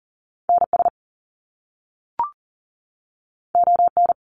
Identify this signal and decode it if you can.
Morse code, heard as a keyed tone, about 40 wpm, 700 Hz. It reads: NS ON
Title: Canadian Provinces and Territories 40wpm